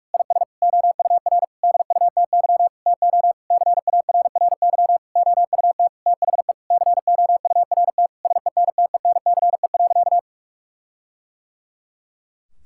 Morse code, heard as a keyed tone, about 46 wpm, 700 Hz. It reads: IS OUR DUTY TO CARRY OUT THE COURT SENTENCE.
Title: Sentences from Top 1600 Words 45wpm